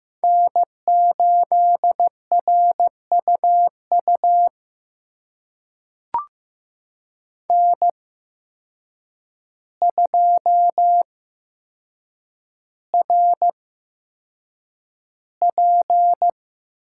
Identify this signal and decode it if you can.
Morse code, heard as a keyed tone, about 15 wpm, spaced as for 7 wpm, 700 Hz. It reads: N8RUU N2RP
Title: USA Call Signs - 5 Characters - 8x Character Spacing - 15wpm